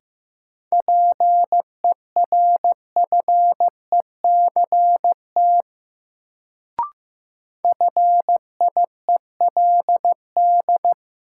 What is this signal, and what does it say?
Morse code, heard as a keyed tone, about 15 wpm, 700 Hz. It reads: PERFECT FIELD
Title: Sentences from Top 1000 Words - New Word Drill - 15wpm